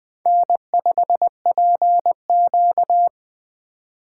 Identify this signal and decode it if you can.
Morse code, heard as a keyed tone, about 21 wpm, 700 Hz. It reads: N5PQ
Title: USA Call Signs - 4x Character Spacing - 20wpm